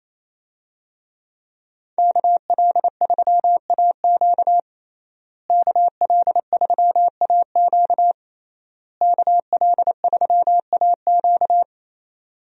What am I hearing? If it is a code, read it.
Morse code, heard as a keyed tone, about 28 wpm, 700 Hz. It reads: KL3AQ KL3AQ KL3AQ
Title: USA Call Signs - 5 Characters - Repeated 3x - 28wpm